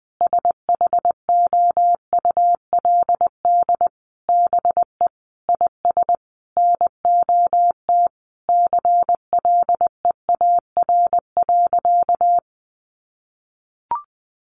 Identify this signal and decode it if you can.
Morse code, heard as a keyed tone, about 19 wpm, 700 Hz. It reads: SHOULD BE IS NOT CLEAR.
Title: Sentences from Top 1300 Words 20wpm